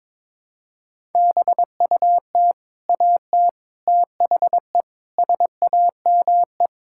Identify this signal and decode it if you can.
Morse code, heard as a keyed tone, about 22 wpm, 700 Hz. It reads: BUT AT THE SAME
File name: practice / 22wpm / Sentences with 4-Word Phrases - 22wpm